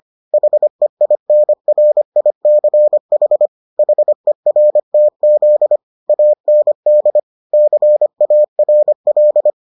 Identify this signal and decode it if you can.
Morse code, heard as a keyed tone, about 25 wpm, 600 Hz. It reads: HEINRICH HERTZ AND CARL